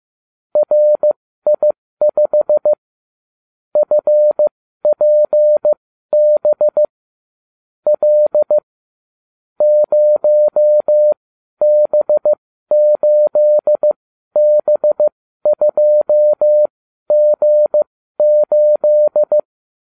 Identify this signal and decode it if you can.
Morse code, heard as a keyed tone, about 15 wpm, spaced as for 12 wpm, 600 Hz. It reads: RI5 FPB L 0B8B2G8